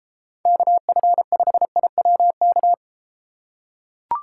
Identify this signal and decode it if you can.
Morse code, heard as a keyed tone, about 30 wpm, 700 Hz. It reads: KF5IWK